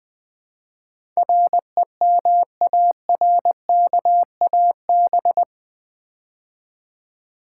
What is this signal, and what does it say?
Morse code, heard as a keyed tone, about 20 wpm, 700 Hz. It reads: REMARKAB
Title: Build A Word - 4 Letters At A Time - 10-Letter Words - 20wpm